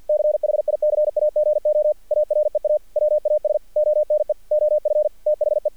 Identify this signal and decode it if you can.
Morse code, heard as a keyed tone, about 48 wpm, 600 Hz. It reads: CHICAGO AREA WAS ONE OF THE
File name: archived / CRSnet